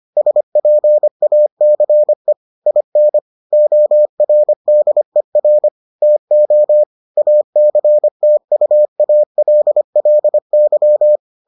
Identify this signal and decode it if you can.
Morse code, heard as a keyed tone, about 25 wpm, 600 Hz. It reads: SPACE IN ORDER TO ACTUALLY